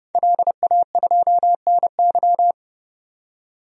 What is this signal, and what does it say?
Morse code, heard as a keyed tone, about 30 wpm, 700 Hz. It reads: LA2NY
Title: DXCC-100 Call Signs - 5 Characters - 30wpm